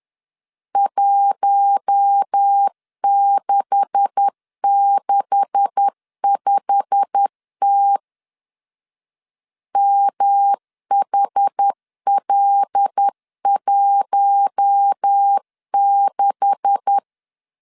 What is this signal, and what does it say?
Morse code, heard as a keyed tone, about 10 wpm, 800 Hz. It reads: 1665T MHL16
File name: LEZIONI / 9 286 10wpm 2735